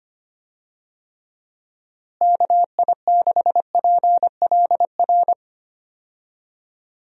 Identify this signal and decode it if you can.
Morse code, heard as a keyed tone, about 25 wpm, 700 Hz. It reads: KI6PLR
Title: USA Call Signs - 6 Characters - Repeated 2x - 25wpm